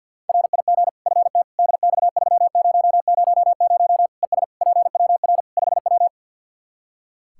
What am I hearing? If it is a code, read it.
Morse code, heard as a keyed tone, about 47 wpm, 700 Hz. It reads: RIG FT DX3000 ES PWR 5W
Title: K1USN SST QSO Practice 50wpm